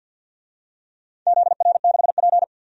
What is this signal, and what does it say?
Morse code, heard as a keyed tone, about 53 wpm, 700 Hz. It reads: ZR6P
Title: DXCC-100 Call Signs - 4 Characters - 50wpm